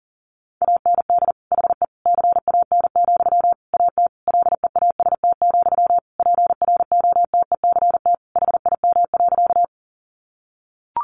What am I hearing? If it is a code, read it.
Morse code, heard as a keyed tone, about 40 wpm, 700 Hz. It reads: AND HE CAN, AT LEAST, PROTECT HIM.
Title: Sentences from Top 900 Words 40wpm